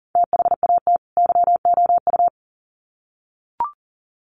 Morse code, heard as a keyed tone, about 40 wpm, 700 Hz. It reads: THAT YOU
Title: Sets of 2 Words 40wpm